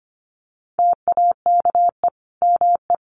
Morse code, heard as a keyed tone, about 25 wpm, 700 Hz. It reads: TAKE ME
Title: Sentences from Top 1400 Words 25wpm